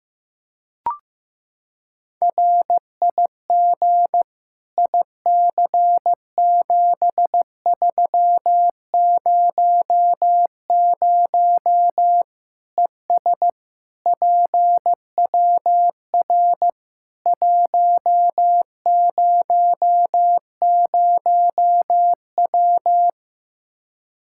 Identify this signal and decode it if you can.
Morse code, heard as a keyed tone, about 15 wpm, 700 Hz. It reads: RIG IC7300 ES PWR 100W
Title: Practice QSO - Exchange 2 - Info - 15wpm